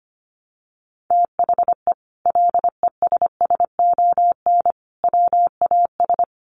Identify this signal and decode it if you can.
Morse code, heard as a keyed tone, about 25 wpm, 700 Hz. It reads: THE LESSON WAS
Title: Sentences from Top 1200 Words 25wpm